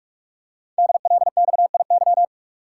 Morse code, heard as a keyed tone, about 44 wpm, 700 Hz. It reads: DLXIY
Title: Code Groups - 5 Letters - 45wpm